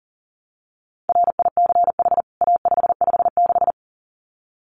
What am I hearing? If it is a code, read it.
Morse code, heard as a keyed tone, about 37 wpm, 700 Hz. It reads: RICH A556
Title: CWT Exchange 40wpm